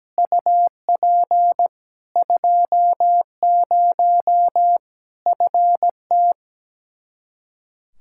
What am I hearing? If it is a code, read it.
Morse code, heard as a keyed tone, about 17 wpm, 700 Hz. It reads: UP 20 FT
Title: Practice QSO - Exchange 2 - Info - 17wpm